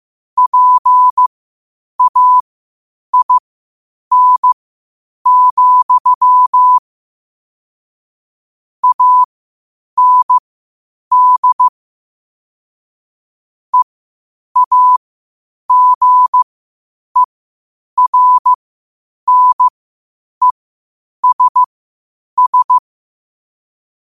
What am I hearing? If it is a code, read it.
Morse code, heard as a keyed tone, about 15 wpm, spaced as for 8 wpm, 1000 Hz. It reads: PAIN, AND EAGERNESS
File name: qotdfiles / 8wpm